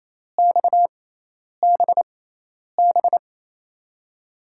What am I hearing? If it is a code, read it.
Morse code, heard as a keyed tone, about 28 wpm, spaced as for 10 wpm, 700 Hz. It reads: XBB
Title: USA Call Signs - 5 Characters - 6x Character Spacing - 28wpm